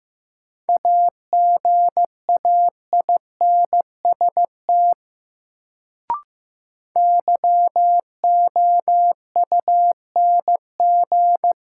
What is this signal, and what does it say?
Morse code, heard as a keyed tone, about 15 wpm, 700 Hz. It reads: AGAINST YOUNG
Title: Sentences from Top 400 Words - New Word Drill - 15wpm